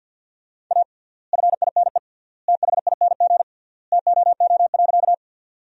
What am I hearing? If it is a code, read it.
Morse code, heard as a keyed tone, about 50 wpm, 700 Hz. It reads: A FINE THING TOO.
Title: 3x Word-Spacing - Sentences from Top 500 Words - 50wpm